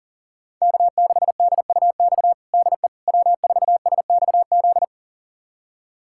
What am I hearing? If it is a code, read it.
Morse code, heard as a keyed tone, about 40 wpm, 700 Hz. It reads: K6DUX DE W4SXZ
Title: USA Callsign DE Callsign - 5 Characters - 40wpm